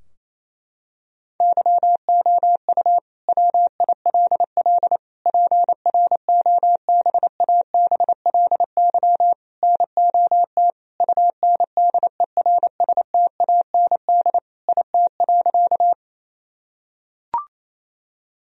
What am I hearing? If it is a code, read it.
Morse code, heard as a keyed tone, about 28 wpm, 700 Hz. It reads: YOU WILL PROBABLY NOT UNDERSTAND IT.
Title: Sentences from Top 1000 Words - Review Part 4 - 28wpm